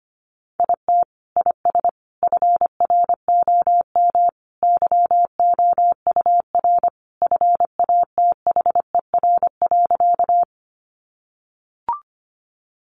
Morse code, heard as a keyed tone, about 25 wpm, 700 Hz. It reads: IT IS FROM YOUR FATHER.
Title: Sentences from Top 300 Words 25wpm